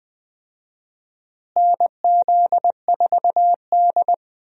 Morse code, heard as a keyed tone, about 20 wpm, 700 Hz. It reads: NZ4D
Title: USA Call Signs - 4 Characters - Repeated 2x - 20wpm